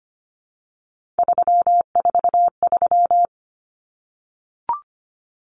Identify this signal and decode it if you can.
Morse code, heard as a keyed tone, about 25 wpm, 700 Hz. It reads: 343
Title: Common Cut Numbers - 3-Digit 25wpm